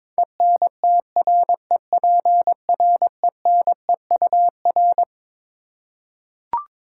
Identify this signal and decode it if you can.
Morse code, heard as a keyed tone, about 21 wpm, 700 Hz. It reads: ENTREPRENEUR
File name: Twelve Letter Words - 22wpm